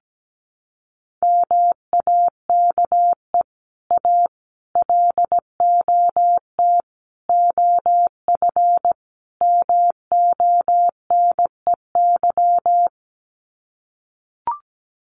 Morse code, heard as a keyed tone, about 17 wpm, 700 Hz. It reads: MAKE A LOT OF MONEY
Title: Sets of 5 Words Encore2 17wpm